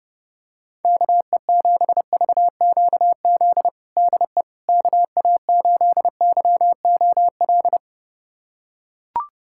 Morse code, heard as a keyed tone, about 32 wpm, 700 Hz. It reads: KE7VQZ DE KA8YOL
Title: USA Callsign DE Callsign - 6 Characters - 30wpm